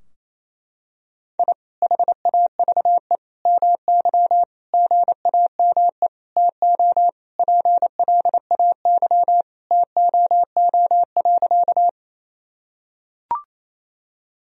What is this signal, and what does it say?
Morse code, heard as a keyed tone, about 28 wpm, 700 Hz. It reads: I HAVE MY GAME TO PLAY TOO.
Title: Sentences from Top 1000 Words - Review Part 3 - 28wpm